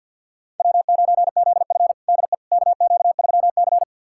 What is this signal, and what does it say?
Morse code, heard as a keyed tone, about 48 wpm, 700 Hz. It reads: W9ZF DE KQ3C
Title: USA Callsign DE Callsign - 4 Characters - 50wpm